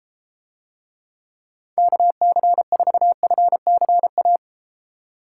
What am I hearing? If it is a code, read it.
Morse code, heard as a keyed tone, about 32 wpm, 700 Hz. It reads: KC4FCA